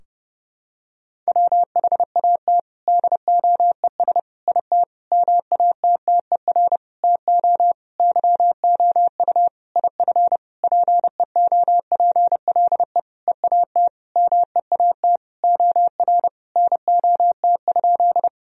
Morse code, heard as a keyed tone, about 30 wpm, 700 Hz. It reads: WHAT DOES IT MATTER TO YOU IF PEOPLE EAT MEAT OR NOT?
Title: Sentences from Top 1000 Words - Review Part 3 - 30wpm